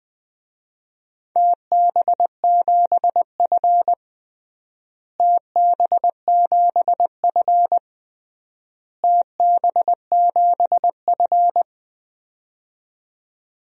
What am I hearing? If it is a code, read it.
Morse code, heard as a keyed tone, about 20 wpm, 700 Hz. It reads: TB7F TB7F TB7F
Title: DXCC-100 Call Signs - 4 Characters - Repeated 3x - 20wpm